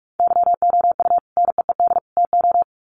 Morse code, heard as a keyed tone, about 44 wpm, 700 Hz. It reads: YOU NEED TO
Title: Sets of 3 Words 45wpm